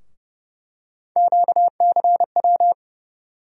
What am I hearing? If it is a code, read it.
Morse code, heard as a keyed tone, about 30 wpm, 700 Hz. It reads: QCW
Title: USA Call Sign Postfixes - 3 Letters 30wpm